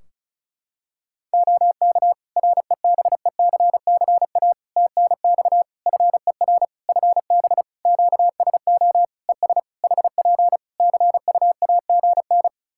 Morse code, heard as a keyed tone, about 34 wpm, 700 Hz. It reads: OK REBECCA TNX FER FB QSO ES HP CUAGN
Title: Practice QSO - Exchange 4 - Ending - 35wpm